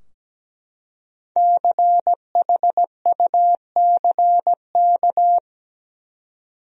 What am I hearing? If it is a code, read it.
Morse code, heard as a keyed tone, about 17 wpm, 700 Hz. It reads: CHUCK
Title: USA Common Names - 17wpm